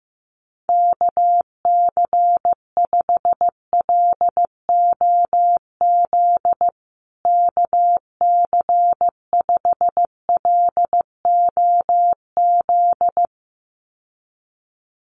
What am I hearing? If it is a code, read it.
Morse code, heard as a keyed tone, about 15 wpm, 700 Hz. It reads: KC5LOZ KC5LOZ
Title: USA Call Signs Repeated 2x 15wpm